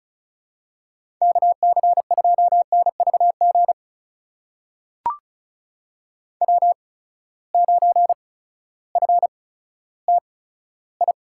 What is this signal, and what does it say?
Morse code, heard as a keyed tone, about 36 wpm, spaced as for 16 wpm, 700 Hz. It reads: KC2NVG W9FTI
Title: USA Call Signs - 8x Character Spacing - 35wpm